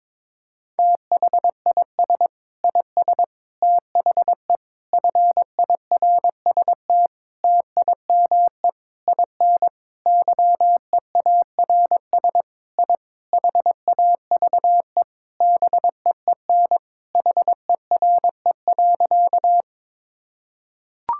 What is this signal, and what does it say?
Morse code, heard as a keyed tone, about 22 wpm, 700 Hz. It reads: THIS IS THE FIRST TIME IN YEARS I HAVE BEEN HERE.
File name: practice / 22wpm / Sentences with 4-Word Phrases - 22wpm